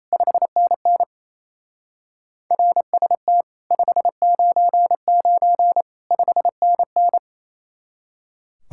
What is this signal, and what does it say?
Morse code, heard as a keyed tone, about 34 wpm, 700 Hz. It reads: 5NN RST 599 5NN
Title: Speed-Racing Common QSO Phrases - 28wpm